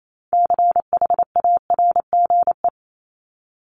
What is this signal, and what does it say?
Morse code, heard as a keyed tone, about 28 wpm, 700 Hz. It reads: CHARGE
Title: Top 751-1000 Words 28wpm